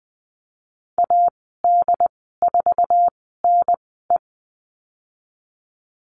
Morse code, heard as a keyed tone, about 20 wpm, spaced as for 14 wpm, 700 Hz. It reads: AD4NE
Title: 161 Lesson 26g - Letter g - Call Signs 20wpm